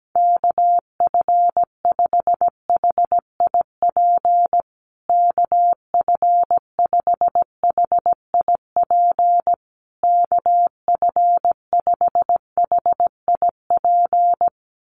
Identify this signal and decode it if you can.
Morse code, heard as a keyed tone, about 17 wpm, 700 Hz. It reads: KF5HIP KF5HIP KF5HIP
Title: USA Call Signs Repeated 3x 17wpm